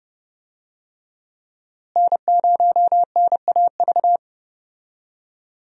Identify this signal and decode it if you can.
Morse code, heard as a keyed tone, about 29 wpm, 700 Hz. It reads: N0NAV
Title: USA Call Signs - 5 Characters - Repeated 3x - 30wpm